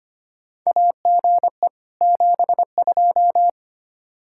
Morse code, heard as a keyed tone, about 25 wpm, 700 Hz. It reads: AGE 72
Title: Practice QSO - Exchange 3 - More Info - 25wpm